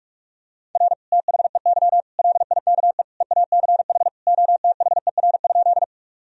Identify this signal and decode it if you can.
Morse code, heard as a keyed tone, about 46 wpm, 700 Hz. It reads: R THEY LIKE EACH OTHER?
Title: Sentences from Top 200 Words and Abbreviations 45wpm